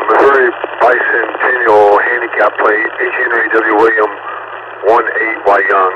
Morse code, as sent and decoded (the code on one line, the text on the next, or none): none